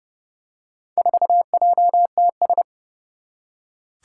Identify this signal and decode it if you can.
Morse code, heard as a keyed tone, about 30 wpm, 700 Hz. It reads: 4JTS
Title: Four Challenging Letter-Number Combinations 30wpm